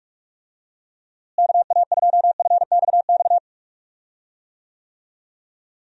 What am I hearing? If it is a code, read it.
Morse code, heard as a keyed tone, about 44 wpm, 700 Hz. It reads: KA2FXX
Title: USA Call Signs - 6 Characters - Repeated 3x - 45wpm